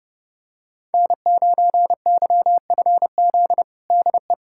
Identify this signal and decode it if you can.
Morse code, heard as a keyed tone, about 30 wpm, 700 Hz. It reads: N9YFZ DE
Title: USA Callsign DE Callsign - 5 Characters - 30wpm